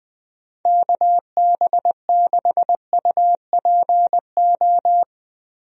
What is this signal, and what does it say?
Morse code, heard as a keyed tone, about 20 wpm, 700 Hz. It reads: KB6UPO
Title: USA Call Signs - 6 Characters - 20wpm